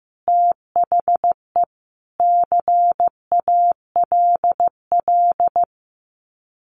Morse code, heard as a keyed tone, about 15 wpm, 700 Hz. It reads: THE CALL
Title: Common QSO Phrases 15wpm